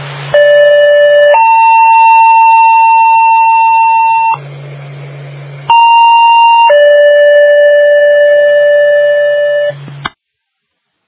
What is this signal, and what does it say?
Two-tone page, heard as a one-second tone, then a three-second tone, about 580 then 900 Hz, about 930 then 580 Hz. It is heard from 0.3 to 9.7 s.